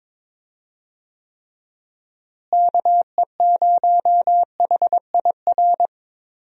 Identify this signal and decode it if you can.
Morse code, heard as a keyed tone, about 22 wpm, 700 Hz. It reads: KE0HIR
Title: USA Call Signs - 6 Characters - Repeated 3x - 22wpm